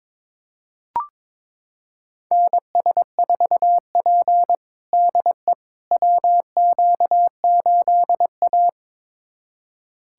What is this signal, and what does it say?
Morse code, heard as a keyed tone, about 22 wpm, 700 Hz. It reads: NS4P DE WQ8A
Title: USA Callsign DE Callsign - 4 Characters - 22wpm